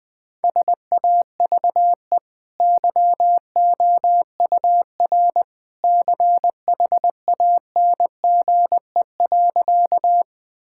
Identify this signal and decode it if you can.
Morse code, heard as a keyed tone, about 20 wpm, 700 Hz. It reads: SAVE YOUR CHANGE.
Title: Easy Sentences with Hints - 20wpm